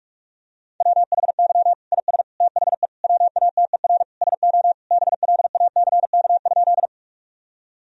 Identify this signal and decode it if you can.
Morse code, heard as a keyed tone, about 46 wpm, 700 Hz. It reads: WHY IS THE WATER SO BLACK?
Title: Sentences from Top 1000 Words - Review Part 2 - 45wpm